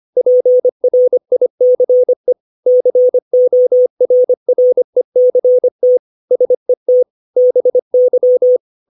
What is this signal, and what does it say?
Morse code, heard as a keyed tone, about 25 wpm, 500 Hz. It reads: PRICE CORRECT SET BY
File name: 632 words / 632 Words 25WPM